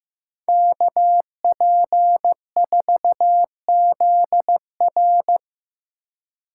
Morse code, heard as a keyed tone, about 15 wpm, 700 Hz. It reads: KP4ZR
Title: USA Call Signs - 5 Characters - 2x Character Spacing - 15wpm